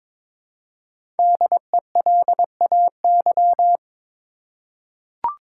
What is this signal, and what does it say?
Morse code, heard as a keyed tone, about 22 wpm, 700 Hz. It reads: DELAY
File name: 5 Letter Words - First 3 Letters Given - 22wpm